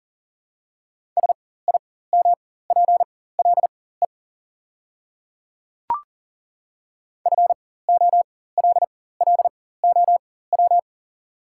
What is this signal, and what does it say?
Morse code, heard as a keyed tone, about 40 wpm, spaced as for 19 wpm, 690 Hz. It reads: SIMPLE FOLLOW
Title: 4x Character Spacing - Top 500 Words - 40wpm